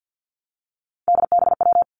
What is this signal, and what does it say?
Morse code, heard as a keyed tone, about 50 wpm, 700 Hz. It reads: DBW
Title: Three Challenging Letter-Number Combinations 50wpm